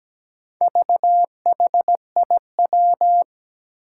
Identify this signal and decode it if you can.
Morse code, heard as a keyed tone, about 17 wpm, 700 Hz. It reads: VHIW
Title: Four Challenging Letter-Number Combinations 17wpm